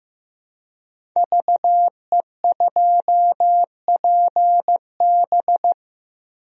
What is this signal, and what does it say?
Morse code, heard as a keyed tone, about 15 wpm, 700 Hz. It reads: VE2PB
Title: DXCC-100 Call Signs - 5 Characters - 15wpm